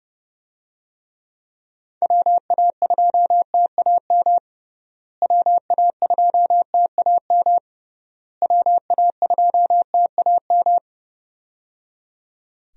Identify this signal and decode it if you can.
Morse code, heard as a keyed tone, about 30 wpm, 700 Hz. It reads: WA2TAM WA2TAM WA2TAM
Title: USA Call Signs - 6 Characters - Repeated 3x - 30wpm